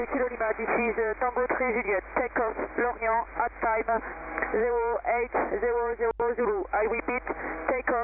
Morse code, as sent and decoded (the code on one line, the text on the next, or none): none